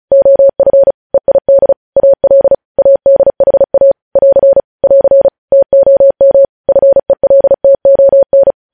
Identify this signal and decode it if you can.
Morse code, heard as a keyed tone, about 36 wpm, 550 Hz. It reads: OF EID AL ADHA <AR> <AR> TOM FELTON
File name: podcast / News headlines-1x-35x35